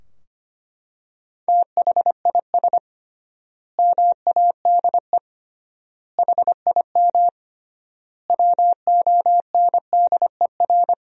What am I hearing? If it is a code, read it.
Morse code, heard as a keyed tone, about 25 wpm, 700 Hz. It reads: THIS MADE HIM WONDER
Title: 3x Word-Spacing - Sentences from Top 500 Words - 25wpm